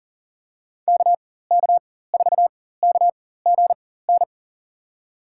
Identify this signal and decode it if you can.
Morse code, heard as a keyed tone, about 40 wpm, spaced as for 19 wpm, 700 Hz. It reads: KK4KGN
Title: USA Call Signs - 6 Characters - 4x Character Spacing - 40wpm